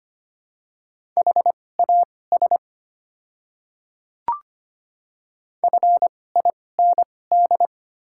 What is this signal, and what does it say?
Morse code, heard as a keyed tone, about 25 wpm, spaced as for 18 wpm, 700 Hz. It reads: HAS FIND
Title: 2x Character Spacing - Top 100 Words - 25wpm